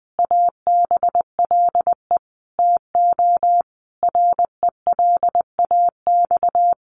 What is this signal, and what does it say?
Morse code, heard as a keyed tone, about 20 wpm, 700 Hz. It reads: ABLE TO RELAX
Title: Sentences from Top 1400 Words 20wpm